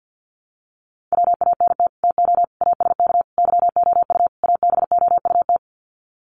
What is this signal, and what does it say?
Morse code, heard as a keyed tone, about 48 wpm, 700 Hz. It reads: WANT TO ASK YOU ABOUT
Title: Sets of 5 Words Encore2 50wpm